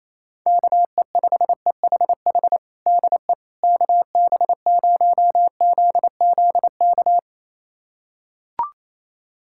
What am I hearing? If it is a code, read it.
Morse code, heard as a keyed tone, about 28 wpm, 700 Hz. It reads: KE5EHH DE KB0ZZK
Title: USA Callsign DE Callsign - 6 Characters - 28wpm